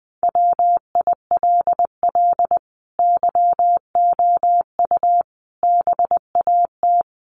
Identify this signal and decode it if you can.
Morse code, heard as a keyed tone, about 20 wpm, 700 Hz. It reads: WILL YOU BAT
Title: Sentences from Top 1500 Words 20wpm